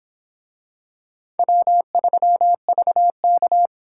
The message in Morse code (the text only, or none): W3VK